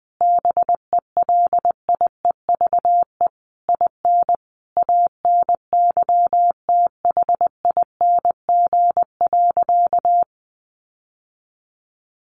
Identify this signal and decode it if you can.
Morse code, heard as a keyed tone, about 20 wpm, 700 Hz. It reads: BELIEVE IN ANYTHING.
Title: Sentences from Top 1500 Words 20wpm